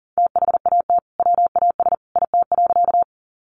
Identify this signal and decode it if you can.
Morse code, heard as a keyed tone, about 40 wpm, 700 Hz. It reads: THAT WAS IT.
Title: Sentences from Top 100 Words 40wpm